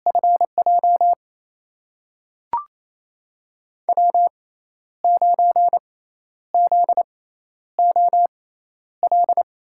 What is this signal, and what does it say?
Morse code, heard as a keyed tone, about 28 wpm, spaced as for 11 wpm, 700 Hz. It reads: FJ W9ZOL
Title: USA Call Signs - 5 Characters - 6x Character Spacing - 28wpm